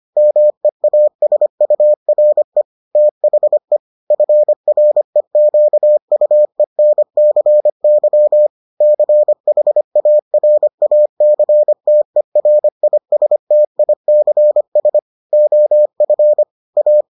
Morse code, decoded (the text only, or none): MEASURE THE FREQUENCY CHARACTERISTICS OF A